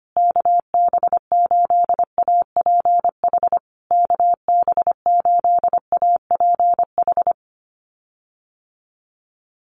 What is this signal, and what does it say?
Morse code, heard as a keyed tone, about 25 wpm, 700 Hz. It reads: KB8APH KB8APH
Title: USA Call Signs Repeated 3x 25wpm